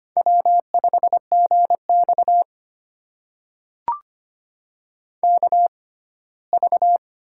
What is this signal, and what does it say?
Morse code, heard as a keyed tone, about 25 wpm, spaced as for 17 wpm, 700 Hz. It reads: W5GX KV
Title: USA Call Signs - 4 Characters - 6x Character Spacing - 25wpm